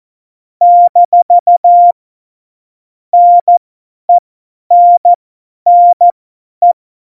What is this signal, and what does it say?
Morse code, heard as a keyed tone, about 14 wpm, spaced as for 10 wpm, 700 Hz. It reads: - NENNE
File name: Morsekurs / Witze / Witz 19. Januar 2021, 50 F70